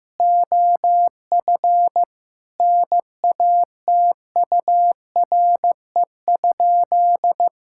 Morse code, heard as a keyed tone, about 15 wpm, 700 Hz. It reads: OF NATURE?
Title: Sentences from Top 1000 Words - Review Part 5 - 15wpm